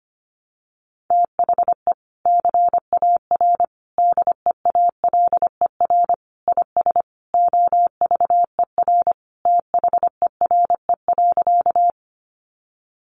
Morse code, heard as a keyed tone, about 25 wpm, 700 Hz. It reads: THE CAR DEALER IS OVER THERE.
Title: Sentences from Top 1100 Words 25wpm